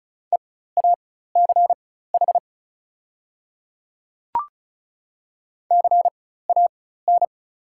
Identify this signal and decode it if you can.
Morse code, heard as a keyed tone, about 36 wpm, spaced as for 16 wpm, 700 Hz. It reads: EACH CAN
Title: 4x Character Spacing - Top 100 Words - 35wpm